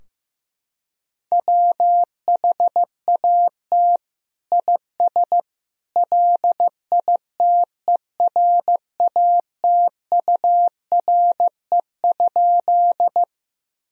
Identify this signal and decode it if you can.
Morse code, heard as a keyed tone, about 15 wpm, 700 Hz. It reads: WHAT IS LITERATURE?
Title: Sentences from Top 1000 Words - Review Part 3 - 15wpm